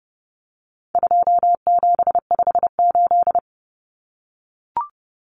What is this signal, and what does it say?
Morse code, heard as a keyed tone, about 30 wpm, 700 Hz. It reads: 2758
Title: Numbers - 4-Digit 30wpm